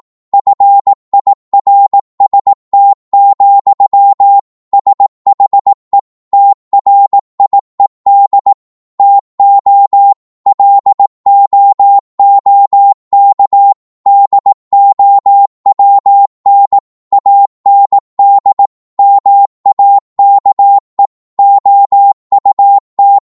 Morse code, heard as a keyed tone, about 18 wpm, 800 Hz. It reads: FIRST, SHE TRIED TO LOOK DOWN AND MAKE OUT
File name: Morse / psg-18wpm